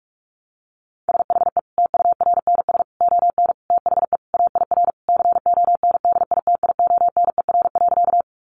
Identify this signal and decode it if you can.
Morse code, heard as a keyed tone, about 44 wpm, 700 Hz. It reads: SHE TURNS ON THE AIR CONDITIONER.
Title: Easy Phrases 45wpm